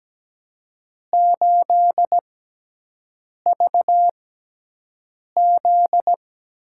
Morse code, heard as a keyed tone, about 17 wpm, spaced as for 6 wpm, 700 Hz. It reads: 8VZ
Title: USA Call Signs - 4 Characters - 6x Character Spacing - 17wpm